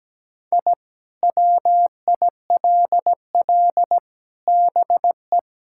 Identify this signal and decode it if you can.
Morse code, heard as a keyed tone, about 17 wpm, 700 Hz. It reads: I WILL BE
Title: Sentences from Top 1000 Words - Review Part 6 - 17wpm